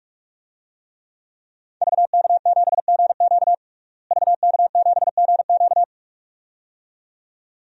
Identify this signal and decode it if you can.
Morse code, heard as a keyed tone, about 41 wpm, 700 Hz. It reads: VK7GQ VK7GQ
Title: DXCC-100 Call Signs - 5 Characters - Repeated 2x - 45wpm